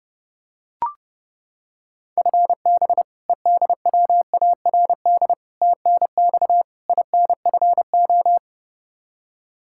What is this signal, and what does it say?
Morse code, heard as a keyed tone, about 30 wpm, 700 Hz. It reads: FB EDWARD TNX INFO
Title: Practice QSO - Exchange 3 - More Info - 30wpm